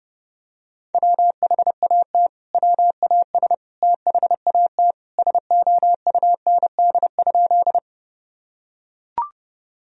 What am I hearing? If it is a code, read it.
Morse code, heard as a keyed tone, about 30 wpm, 700 Hz. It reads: WHAT WAS THAT SOUND?
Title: Sentences from Top 1000 Words - Review Part 1 - 30wpm